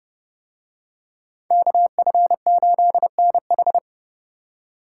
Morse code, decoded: KF8NH

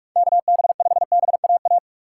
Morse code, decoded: KB5BAA

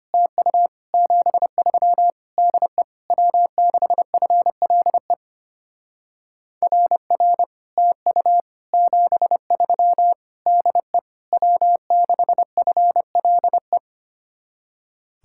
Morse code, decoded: TU 73 DE W6FLE RR TU 73 DE W6FLE

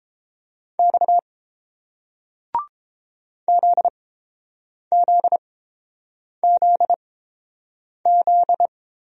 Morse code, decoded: X ZZZZ

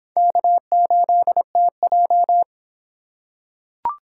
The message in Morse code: K8TJ

